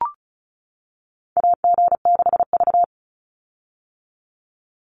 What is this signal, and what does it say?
Morse code, heard as a keyed tone, about 36 wpm, 700 Hz. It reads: AG6V
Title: Call Signs - 35wpm